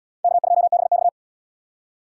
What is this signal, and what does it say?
Morse code, heard as a keyed tone, about 73 wpm, 700 Hz. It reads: K2BX